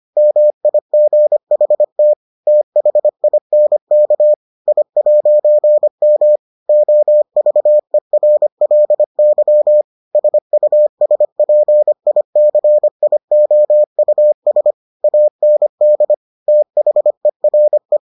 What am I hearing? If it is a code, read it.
Morse code, heard as a keyed tone, about 25 wpm, 600 Hz. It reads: MIGHT THINK I'M OVERLY SUSPICIOUS AND THERE